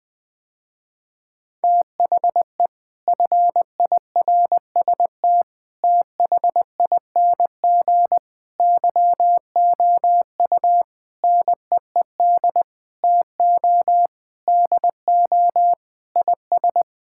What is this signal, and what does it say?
Morse code, heard as a keyed tone, about 21 wpm, 700 Hz. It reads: THE FIRST THING YOU NEED TO DO IS
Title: Sentences with 8-Word Phrases - Spoken Beginning - 20wpm